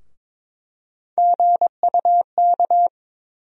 Morse code, heard as a keyed tone, about 22 wpm, 700 Hz. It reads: GUK
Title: USA Call Sign Postfixes - 3 Letters 22wpm